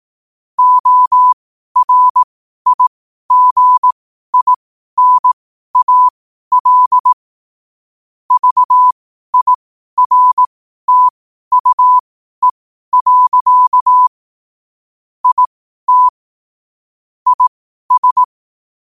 Morse code, heard as a keyed tone, about 18 wpm, spaced as for 12 wpm, 1000 Hz. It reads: ORIGINAL VIRTUE. IT IS
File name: qotdfiles / 12wpm